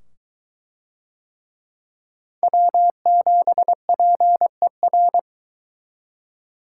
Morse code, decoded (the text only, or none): W7PER